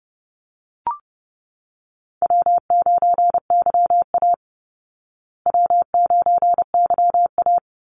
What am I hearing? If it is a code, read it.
Morse code, heard as a keyed tone, about 30 wpm, 700 Hz. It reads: W9YA W9YA
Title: Call Signs Repeated 2x - 30wpm